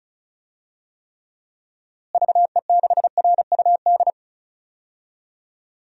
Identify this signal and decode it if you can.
Morse code, heard as a keyed tone, about 36 wpm, 700 Hz. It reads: VE6RUD
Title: DXCC-100 Call Signs - 6 Characters - Repeated 3x - 35wpm